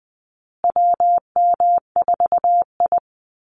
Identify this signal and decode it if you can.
Morse code, heard as a keyed tone, about 20 wpm, 700 Hz. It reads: WM4I
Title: Call Signs Repeated 3x - 20wpm